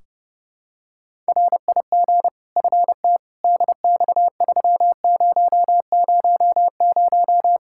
Morse code, heard as a keyed tone, about 30 wpm, 700 Hz. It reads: RIG FT DX3000
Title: Practice QSO - Exchange 2 - Info - 30wpm